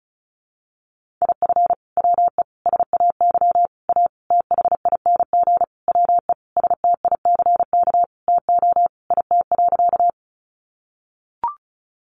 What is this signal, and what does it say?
Morse code, heard as a keyed tone, about 34 wpm, 700 Hz. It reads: IF WE SAY A THING WE STICK TO IT.
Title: Sentences from Top 1200 Words 35wpm